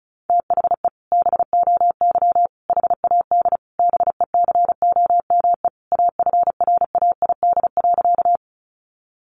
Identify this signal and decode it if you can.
Morse code, heard as a keyed tone, about 34 wpm, 700 Hz. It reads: THE BOY HAD BECOME AFRAID.
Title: Sentences from Top 1500 Words 35wpm